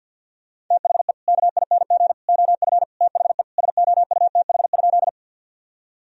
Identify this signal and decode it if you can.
Morse code, heard as a keyed tone, about 50 wpm, 700 Hz. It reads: THE KING OF THE SOUTH?
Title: Sentences from Top 1000 Words - Review Part 2 - 50wpm